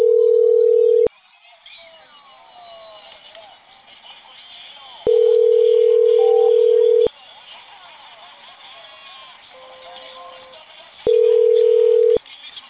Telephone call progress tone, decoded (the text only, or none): ringback tone